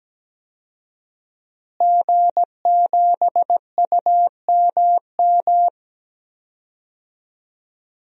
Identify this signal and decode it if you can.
Morse code, heard as a keyed tone, about 17 wpm, 700 Hz. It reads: G7UMM
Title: DXCC-100 Call Signs - 5 Characters - Repeated 2x - 17wpm